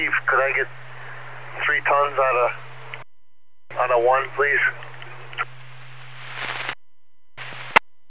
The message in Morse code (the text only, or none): none